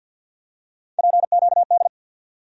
Morse code, heard as a keyed tone, about 47 wpm, 700 Hz. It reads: PQD
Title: USA Call Sign Postfixes - 3 Letters 50wpm